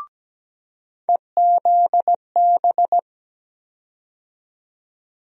Morse code, heard as a keyed tone, about 17 wpm, 700 Hz. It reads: EZB
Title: USA Call Sign Postfixes - 3 Letters 17wpm